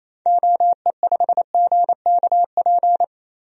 Morse code, heard as a keyed tone, about 28 wpm, 700 Hz. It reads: OE5GKP
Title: DXCC-100 Call Signs - 6 Characters - 28wpm